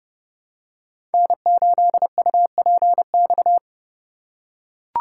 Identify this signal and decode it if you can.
Morse code, heard as a keyed tone, about 30 wpm, 700 Hz. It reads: N8UPX
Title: USA Call Signs - 4x Character Spacing - 30wpm